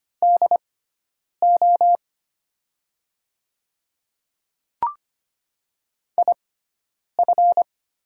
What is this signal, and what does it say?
Morse code, heard as a keyed tone, about 25 wpm, spaced as for 9 wpm, 700 Hz. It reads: DO IF
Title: 6x Character Spacing - Top 100 Words - 25wpm